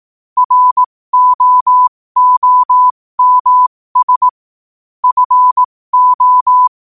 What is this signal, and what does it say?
Morse code, heard as a keyed tone, about 18 wpm, spaced as for 15 wpm, 1000 Hz. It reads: ROOMS FO
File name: qotdfiles / 15wpm